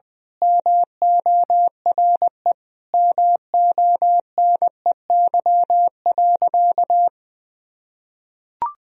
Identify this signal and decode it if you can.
Morse code, heard as a keyed tone, about 20 wpm, 700 Hz. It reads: MORE MONEY.
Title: Sentences from Top 1000 Words - Review Part 4 - 20wpm